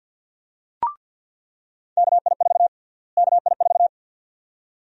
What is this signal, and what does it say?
Morse code, heard as a keyed tone, about 48 wpm, 700 Hz. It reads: KI4 KI4
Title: USA Call Sign Prefixes - Repeated 2x - 50wpm